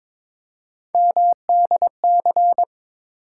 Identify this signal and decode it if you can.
Morse code, heard as a keyed tone, about 21 wpm, 700 Hz. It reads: MDC
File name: practice / 22wpm / ARRL Sections v2 - 22wpm